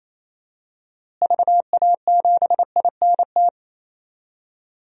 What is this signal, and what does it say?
Morse code, heard as a keyed tone, about 28 wpm, 700 Hz. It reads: VA7INT
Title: DXCC-100 Call Signs - 6 Characters - Repeated 3x - 28wpm